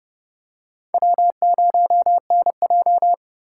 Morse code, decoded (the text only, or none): W0NJ